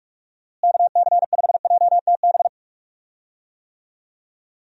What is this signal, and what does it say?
Morse code, heard as a keyed tone, about 46 wpm, 700 Hz. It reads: KC5JTB
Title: USA Call Signs - 6 Characters - Repeated 3x - 45wpm